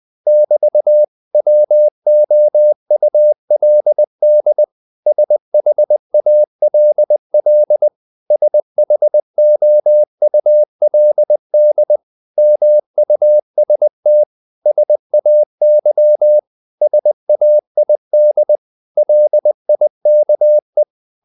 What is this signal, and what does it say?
Morse code, heard as a keyed tone, about 20 wpm, 600 Hz. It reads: <BT>WOULD SHALL SHOULD MUST SAY SAID LIKE
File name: CW / Practice / 500MostCommonEnglishWords20